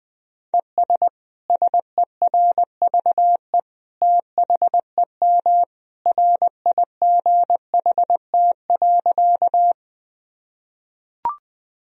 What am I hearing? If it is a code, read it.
Morse code, heard as a keyed tone, about 21 wpm, 700 Hz. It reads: ES SERVE THEM RIGHT.
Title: Sentences from Top 400 Words and Abbreviations 20wpm